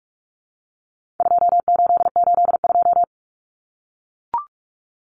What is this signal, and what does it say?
Morse code, heard as a keyed tone, about 44 wpm, 700 Hz. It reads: 2882
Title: Common Cut Numbers - 4-Digit 45wpm